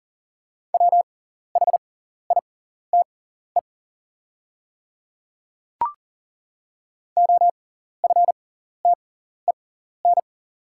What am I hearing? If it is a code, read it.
Morse code, heard as a keyed tone, about 37 wpm, spaced as for 14 wpm, 690 Hz. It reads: WHITE OFTEN
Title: 6x Character Spacing - Top 500 Words - 40wpm